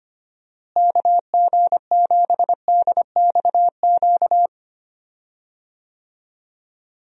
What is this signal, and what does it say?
Morse code, heard as a keyed tone, about 25 wpm, 700 Hz. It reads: KG7DXQ
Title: USA Call Signs - 6 Characters - Repeated 2x - 25wpm